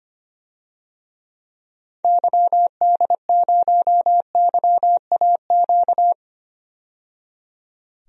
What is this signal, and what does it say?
Morse code, heard as a keyed tone, about 25 wpm, 700 Hz. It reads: YD0YAQ
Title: DXCC-100 Call Signs - 6 Characters - 25wpm